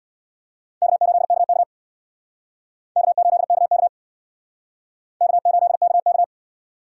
Morse code, heard as a keyed tone, about 62 wpm, 700 Hz. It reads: K8KK K8KK K8KK
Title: Speed Racing - USA 1x2 Call Signs 50wpm